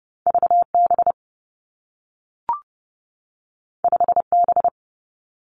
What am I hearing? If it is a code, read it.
Morse code, heard as a keyed tone, about 30 wpm, 700 Hz. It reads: V B 5 B
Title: Two Challenging Letter-Number Combinations 30wpm